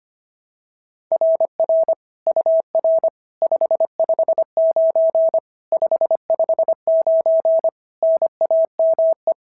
Morse code, heard as a keyed tone, about 24 wpm, 650 Hz. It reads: RR UR 559 559 NAME
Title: SOTA QSO Practice 25wpm